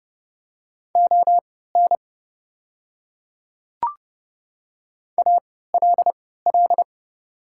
Morse code, heard as a keyed tone, about 30 wpm, spaced as for 17 wpm, 700 Hz. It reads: ON ALL
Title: 3x Character Spacing - Top 100 Words - 30wpm